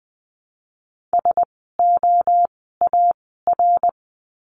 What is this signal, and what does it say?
Morse code, heard as a keyed tone, about 20 wpm, 700 Hz. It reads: S O A R
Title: 061 Lesson 11f - Letter h - Words 20wpm